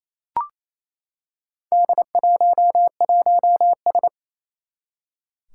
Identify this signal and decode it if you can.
Morse code, heard as a keyed tone, about 27 wpm, 700 Hz. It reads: D11S
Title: Four Challenging Letter-Number Combinations 28wpm